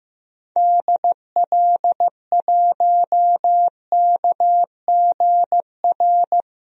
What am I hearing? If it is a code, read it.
Morse code, heard as a keyed tone, about 15 wpm, 700 Hz. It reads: DL1KGR
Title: DXCC-100 Call Signs - 6 Characters - Repeated 3x - 15wpm